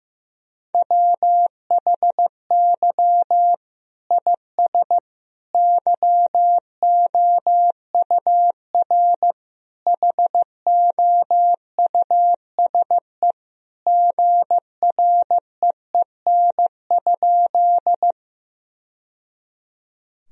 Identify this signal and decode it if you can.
Morse code, heard as a keyed tone, about 16 wpm, 700 Hz. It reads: WHY IS YOUR HOUSE GREEN?
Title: Sentences from Top 1000 Words - Review Part 2 - 15wpm